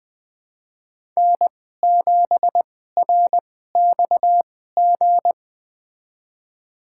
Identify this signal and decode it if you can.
Morse code, heard as a keyed tone, about 20 wpm, spaced as for 15 wpm, 700 Hz. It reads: N7RXG
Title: USA Call Signs - 5 Characters - 2x Character Spacing - 20wpm